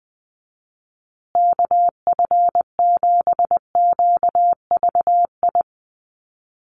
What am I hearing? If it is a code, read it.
Morse code, heard as a keyed tone, about 20 wpm, 700 Hz. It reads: KF7QVI
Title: USA Call Signs Repeated 3x 20wpm